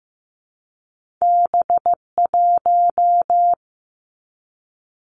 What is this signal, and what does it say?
Morse code, heard as a keyed tone, about 15 wpm, 700 Hz. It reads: B1
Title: Two Challenging Letter-Number Combinations 15wpm